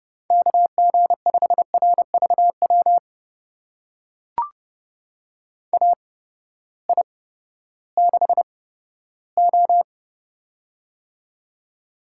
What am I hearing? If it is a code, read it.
Morse code, heard as a keyed tone, about 30 wpm, spaced as for 15 wpm, 700 Hz. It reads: KG5RVW AI6O
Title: USA Call Signs - 8x Character Spacing - 30wpm